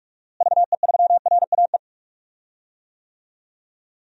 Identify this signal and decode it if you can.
Morse code, heard as a keyed tone, about 43 wpm, 700 Hz. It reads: VE3RAE
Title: DXCC-100 Call Signs - 6 Characters - Repeated 2x - 45wpm